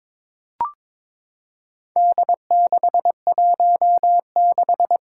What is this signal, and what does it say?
Morse code, heard as a keyed tone, about 23 wpm, 700 Hz. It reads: D616
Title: Four Challenging Letter-Number Combinations 22wpm